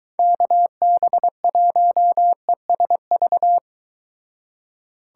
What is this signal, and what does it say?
Morse code, heard as a keyed tone, about 23 wpm, 700 Hz. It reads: KB1ESV